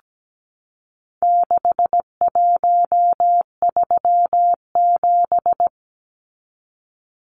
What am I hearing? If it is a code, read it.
Morse code, heard as a keyed tone, about 17 wpm, 700 Hz. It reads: 6137
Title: Numbers - 4-Digit 17wpm